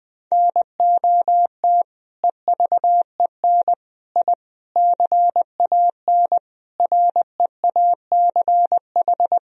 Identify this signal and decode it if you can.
Morse code, heard as a keyed tone, about 20 wpm, 700 Hz. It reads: NOT EVEN I CAN REACH
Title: Sentences from Top 1000 Words - Review Part 2 - 20wpm